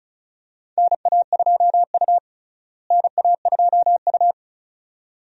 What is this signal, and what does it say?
Morse code, heard as a keyed tone, about 36 wpm, 700 Hz. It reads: NA2U NA2U
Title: USA Call Signs - 4 Characters - Repeated 2x - 35wpm